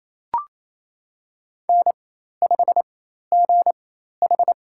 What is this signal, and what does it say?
Morse code, heard as a keyed tone, about 28 wpm, spaced as for 13 wpm, 700 Hz. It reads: N5GH
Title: USA Call Signs - 4 Characters - 4x Character Spacing - 28wpm